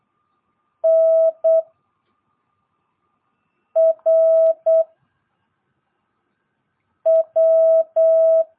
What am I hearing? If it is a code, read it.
Morse code, heard as a keyed tone, about 8 wpm, 660 Hz. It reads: N R W
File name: jh8cbh / 2022cw